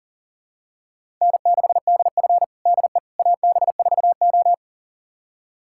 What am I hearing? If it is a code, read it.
Morse code, heard as a keyed tone, about 37 wpm, 700 Hz. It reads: N6DF DE AB4O
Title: USA Callsign DE Callsign - 4 Characters - 40wpm